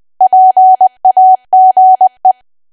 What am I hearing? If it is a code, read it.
Morse code, heard as a keyed tone, about 20 wpm, 750 Hz. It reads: PAGE
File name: Archive / 20 WPM / 20WPM